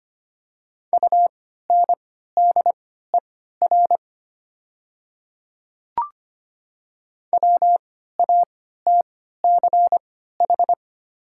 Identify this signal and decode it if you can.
Morse code, heard as a keyed tone, about 25 wpm, spaced as for 14 wpm, 690 Hz. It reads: UNDER WATCH
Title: 3x Character Spacing - Top 300 Words - 25wpm